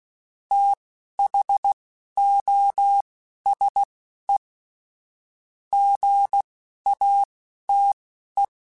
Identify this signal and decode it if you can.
Morse code, heard as a keyed tone, about 16 wpm, spaced as for 11 wpm, 800 Hz. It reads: THOSE GATE
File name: Technical Subjects / Technical Subjects-mp3s / AND Gate @10